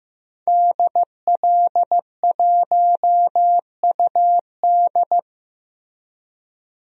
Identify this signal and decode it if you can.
Morse code, heard as a keyed tone, about 15 wpm, 700 Hz. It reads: DL1UD
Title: DXCC-100 Call Signs - 5 Characters - Repeated 3x - 15wpm